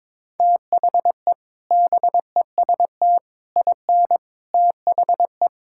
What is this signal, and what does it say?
Morse code, heard as a keyed tone, about 22 wpm, 700 Hz. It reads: THE BEST IN THE
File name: practice / 22wpm / Sentences with 3-Word Phrases - 22wpm